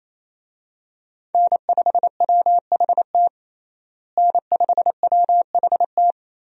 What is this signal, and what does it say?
Morse code, heard as a keyed tone, about 27 wpm, 700 Hz. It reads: N5WHT N5WHT
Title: USA Call Signs - 5 Characters - Repeated 2x - 28wpm